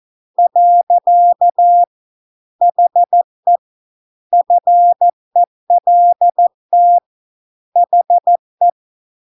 Morse code, heard as a keyed tone, about 14 wpm, 700 Hz. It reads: . HE FELT HE